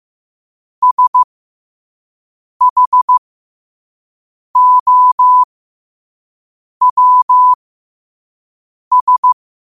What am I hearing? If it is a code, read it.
Morse code, heard as a keyed tone, about 15 wpm, spaced as for 5 wpm, 1000 Hz. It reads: SHOWS
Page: New quotes every day in morse code at 5 Words per minute.